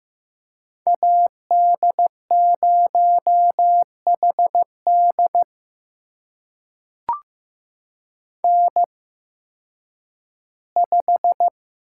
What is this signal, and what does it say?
Morse code, heard as a keyed tone, about 15 wpm, spaced as for 10 wpm, 700 Hz. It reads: AD0HD N5
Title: USA Call Signs - 8x Character Spacing - 15wpm